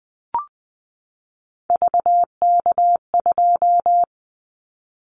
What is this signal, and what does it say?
Morse code, heard as a keyed tone, about 20 wpm, 700 Hz. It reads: VK2
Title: Callsign Prefixes for Top 50 DXCC - 20wpm